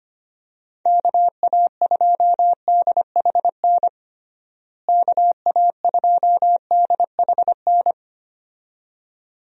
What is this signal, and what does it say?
Morse code, heard as a keyed tone, about 25 wpm, 700 Hz. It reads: KA2DHN KA2DHN
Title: USA Call Signs - 6 Characters - Repeated 2x - 25wpm